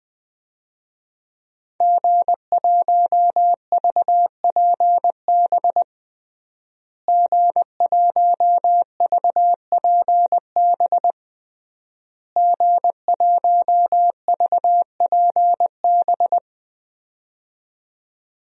Morse code, heard as a keyed tone, about 21 wpm, 700 Hz. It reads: G1VPB G1VPB G1VPB
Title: DXCC-100 Call Signs - 5 Characters - Repeated 3x - 20wpm